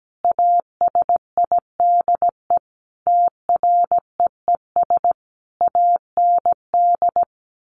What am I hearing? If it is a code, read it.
Morse code, heard as a keyed tone, about 17 wpm, 700 Hz. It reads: ASIDE TREES AND
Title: A Sound Of Thunder 17wpm